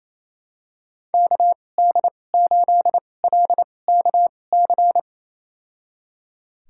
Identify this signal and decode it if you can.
Morse code, heard as a keyed tone, about 28 wpm, spaced as for 20 wpm, 700 Hz. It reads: KD8LKC
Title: USA Call Signs - 2x Character Spacing - 28wpm